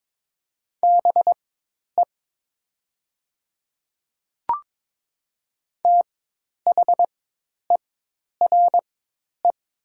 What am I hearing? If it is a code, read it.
Morse code, heard as a keyed tone, about 22 wpm, spaced as for 10 wpm, 700 Hz. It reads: BE THERE
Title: 4x Character Spacing - Top 100 Words - 22wpm